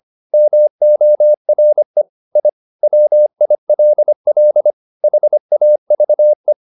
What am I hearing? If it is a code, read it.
Morse code, heard as a keyed tone, about 25 wpm, 600 Hz. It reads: MORE I WILL HAVE